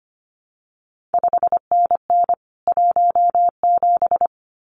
Morse code, heard as a keyed tone, about 25 wpm, 700 Hz. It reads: 5NN 17
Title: CQ WW DX Contest Exchange Encore - 25wpm